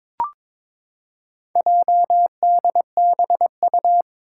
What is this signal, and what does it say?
Morse code, heard as a keyed tone, about 22 wpm, 700 Hz. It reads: JDBU
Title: Four Challenging Letter-Number Combinations 22wpm